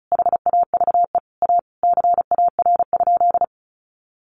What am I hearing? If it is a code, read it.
Morse code, heard as a keyed tone, about 34 wpm, 700 Hz. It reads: HAVE A CAR?
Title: Sentences from Top 300 Words 35wpm